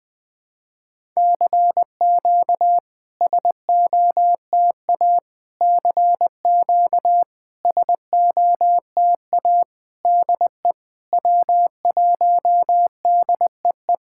Morse code, heard as a keyed tone, about 20 wpm, 700 Hz. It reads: CQ SOTA CQ SOTA DE W1DEE
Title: SOTA QSO Practice 20wpm